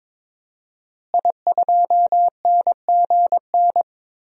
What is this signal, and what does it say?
Morse code, heard as a keyed tone, about 22 wpm, 700 Hz. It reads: I2NGN
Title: DXCC-100 Call Signs - 5 Characters - Repeated 2x - 22wpm